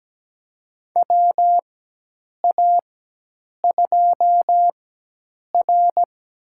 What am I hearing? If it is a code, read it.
Morse code, heard as a keyed tone, about 17 wpm, spaced as for 8 wpm, 700 Hz. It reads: WA2R
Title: USA Call Signs - 4x Character Spacing - 17wpm